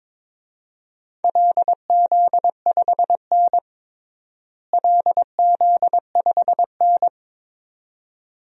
Morse code, heard as a keyed tone, about 22 wpm, 700 Hz. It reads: LZ5N LZ5N
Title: DXCC-100 Call Signs - 4 Characters - Repeated 2x - 22wpm